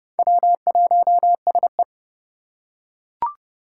Morse code, heard as a keyed tone, about 27 wpm, 700 Hz. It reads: W1SE